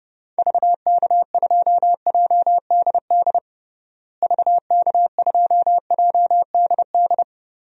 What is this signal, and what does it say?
Morse code, heard as a keyed tone, about 30 wpm, 700 Hz. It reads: VK2JDD VK2JDD
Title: DXCC-100 Call Signs - 6 Characters - Repeated 2x - 30wpm